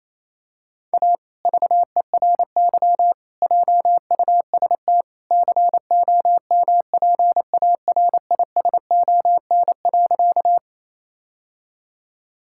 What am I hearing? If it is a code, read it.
Morse code, heard as a keyed tone, about 28 wpm, 700 Hz. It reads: A VERY JUST COMPARISON.
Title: Sentences from Top 1600 Words 28wpm